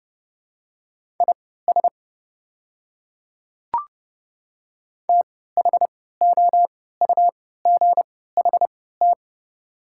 Morse code, heard as a keyed tone, about 31 wpm, spaced as for 17 wpm, 700 Hz. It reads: IS THOUGHT
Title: 3x Character Spacing - Top 500 Words - 30wpm